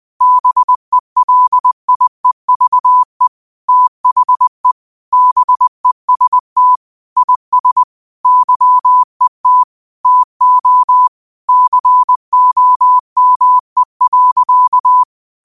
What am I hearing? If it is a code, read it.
Morse code, heard as a keyed tone, about 20 wpm, 1000 Hz. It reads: BELIEVE THE BEST IS YET TO COME.
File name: qotdfiles / 20wpm